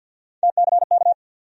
Morse code, heard as a keyed tone, about 53 wpm, 700 Hz. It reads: TCK
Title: USA Call Sign Postfixes - 3 Letters 50wpm